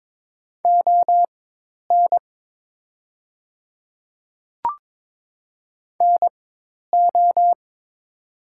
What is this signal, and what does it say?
Morse code, heard as a keyed tone, about 22 wpm, spaced as for 10 wpm, 700 Hz. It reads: ON NO
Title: 4x Character Spacing - Top 100 Words - 22wpm